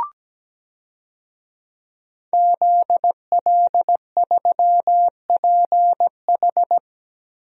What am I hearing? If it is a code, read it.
Morse code, heard as a keyed tone, about 17 wpm, 700 Hz. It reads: ZL3PH
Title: DXCC-100 Call Signs - 5 Characters - Repeated 2x - 17wpm